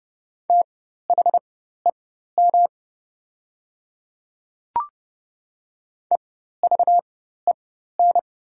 Morse code, heard as a keyed tone, about 31 wpm, spaced as for 14 wpm, 700 Hz. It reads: THEM EVEN
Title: 4x Character Spacing - Top 200 Words - 30wpm